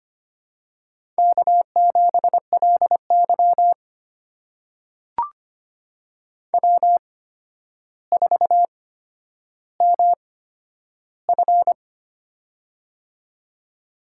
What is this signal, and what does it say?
Morse code, heard as a keyed tone, about 25 wpm, spaced as for 11 wpm, 700 Hz. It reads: K7LY W4MF
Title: USA Call Signs - 4 Characters - 8x Character Spacing - 25wpm